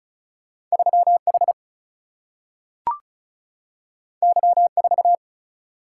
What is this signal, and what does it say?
Morse code, heard as a keyed tone, about 36 wpm, 700 Hz. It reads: 3 H Y 4
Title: Single-Letter Single-Number 35wpm